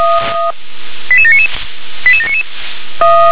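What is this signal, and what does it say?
Five-tone selective call (ZVEI), heard as quick runs of five tones, tones of about 70 ms. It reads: 88799 88799
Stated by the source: ZVEI five-tone call 88795 received by SDR.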